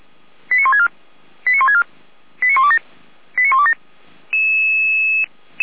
Five-tone selective call (ZVEI), heard as quick runs of five tones, tones of about 70 ms. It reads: 89164 89164 89127 89127
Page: ZVEI